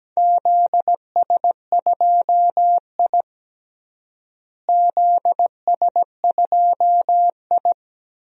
Morse code, heard as a keyed tone, about 17 wpm, 700 Hz. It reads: ZS2I ZS2I
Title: DXCC-100 Call Signs - 4 Characters - Repeated 2x - 17wpm